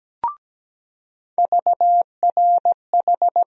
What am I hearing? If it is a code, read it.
Morse code, heard as a keyed tone, about 17 wpm, 700 Hz. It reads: VRH
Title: USA Call Sign Postfixes - 3 Letters 17wpm